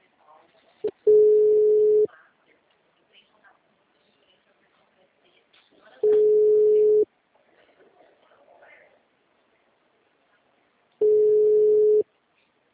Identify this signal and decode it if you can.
Ringback tone (Europe, 425 Hz), with speech.